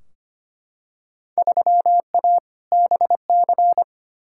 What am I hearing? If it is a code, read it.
Morse code, heard as a keyed tone, about 25 wpm, 700 Hz. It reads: 3A BC
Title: Field Day Exchanges - 25wpm